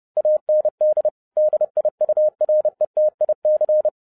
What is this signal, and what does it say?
Morse code, heard as a keyed tone, about 30 wpm, 610 Hz. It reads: AND DIURETIC